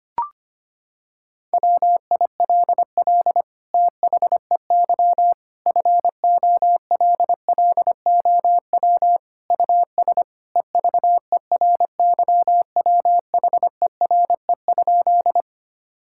Morse code, heard as a keyed tone, about 25 wpm, 700 Hz. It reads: WILL THEY FOLLOW US EVERYWHERE?
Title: Sentences from Top 1600 Words 25wpm